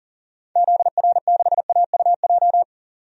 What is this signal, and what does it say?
Morse code, heard as a keyed tone, about 40 wpm, 700 Hz. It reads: ZR6AUJ
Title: DXCC-100 Call Signs - 6 Characters - 40wpm